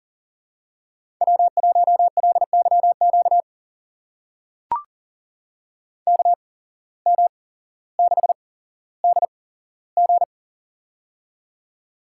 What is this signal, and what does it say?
Morse code, heard as a keyed tone, about 41 wpm, spaced as for 17 wpm, 700 Hz. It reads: W1LYQ KM6DG
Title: USA Call Signs - 5 Characters - 8x Character Spacing - 40wpm